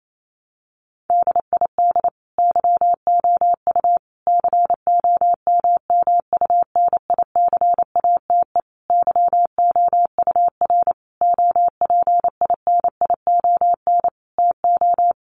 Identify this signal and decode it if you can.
Morse code, heard as a keyed tone, about 28 wpm, 700 Hz. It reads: DID YOU COMMUNICATE YOUR OPINION TO
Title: Sentences from Top 1300 Words 28wpm